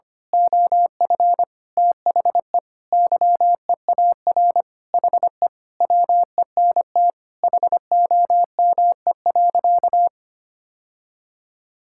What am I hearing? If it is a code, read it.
Morse code, heard as a keyed tone, about 25 wpm, 700 Hz. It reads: OF THE YEAR HE WENT HOME.
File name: practice / 25wpm / Sentences with 4-Word Phrases - 25wpm